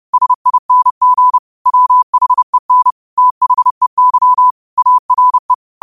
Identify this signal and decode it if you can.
Morse code, heard as a keyed tone, about 30 wpm, 1000 Hz. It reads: SING WHEN THEY ARE
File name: 30wpm